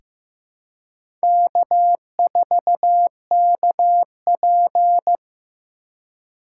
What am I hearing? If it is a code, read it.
Morse code, heard as a keyed tone, about 15 wpm, 700 Hz. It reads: K4KP
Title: USA Call Signs - 4x Character Spacing - 15wpm